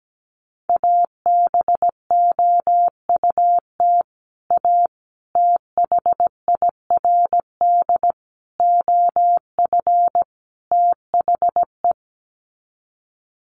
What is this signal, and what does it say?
Morse code, heard as a keyed tone, about 17 wpm, 700 Hz. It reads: ABOUT A THIRD OF THE
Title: Sets of 5 Words Encore2 17wpm